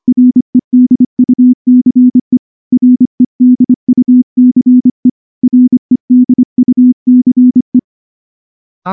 Morse code, heard as a keyed tone, about 24 wpm, 250 Hz. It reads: REDUCE REDUCE REDUCE
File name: morse-training-25WPM